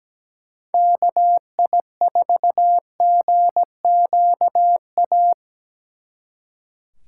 Morse code, heard as a keyed tone, about 20 wpm, 700 Hz. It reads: KI4GQA
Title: Speed Racing - USA 2x3 Call Signs 17wpm